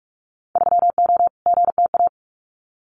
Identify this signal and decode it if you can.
Morse code, heard as a keyed tone, about 46 wpm, 700 Hz. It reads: 3O GTA
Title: Winter Field Day Exchanges Encore 45wpm